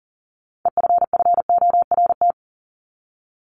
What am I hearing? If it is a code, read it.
Morse code, heard as a keyed tone, about 40 wpm, 700 Hz. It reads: EFFORT
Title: Top 1001-1250 Words 40wpm